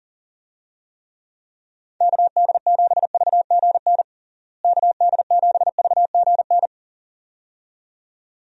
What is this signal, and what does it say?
Morse code, heard as a keyed tone, about 40 wpm, 700 Hz. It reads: KD7VGN KD7VGN
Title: USA Call Signs - 6 Characters - Repeated 2x - 40wpm